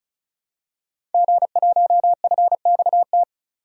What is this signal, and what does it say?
Morse code, heard as a keyed tone, about 36 wpm, 700 Hz. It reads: G1FXT
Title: DXCC-100 Call Signs - 5 Characters - 35wpm